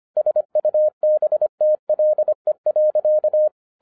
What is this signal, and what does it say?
Morse code, heard as a keyed tone, about 25 wpm, 610 Hz. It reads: SUBTLE.
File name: 2011-07-mp3 / n34 THE 3.11 2 25wpm